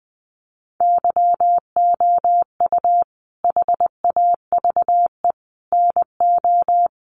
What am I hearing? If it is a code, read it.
Morse code, heard as a keyed tone, about 19 wpm, 700 Hz. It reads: YOU HAVE NO
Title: Sentences from Top 1500 Words 20wpm